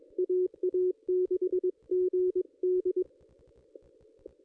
Morse code, heard as a keyed tone, about 22 wpm, 370 Hz. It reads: AA6GD
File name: contests / ss-cw-k5tr-op-n5rz -1800